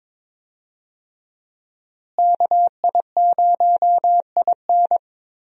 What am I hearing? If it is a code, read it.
Morse code, heard as a keyed tone, about 22 wpm, 700 Hz. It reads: KI0IN
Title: USA Call Signs - 5 Characters - Repeated 2x - 22wpm